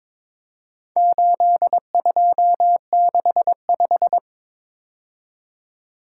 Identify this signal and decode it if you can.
Morse code, heard as a keyed tone, about 22 wpm, 700 Hz. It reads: 8265
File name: practice / 22wpm / Four Letter-Number Combinations - 22wpm